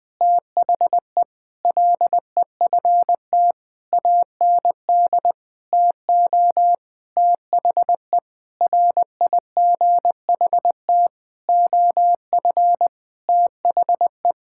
Morse code, heard as a keyed tone, about 20 wpm, 700 Hz. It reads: THE LEFT AND TO THE RIGHT OF THE
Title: Sentences with 8-Word Phrases - 20wpm